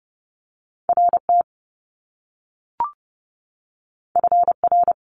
Morse code, heard as a keyed tone, about 30 wpm, 700 Hz. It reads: RT FR
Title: Top Two Letter Combinations in Words 30wpm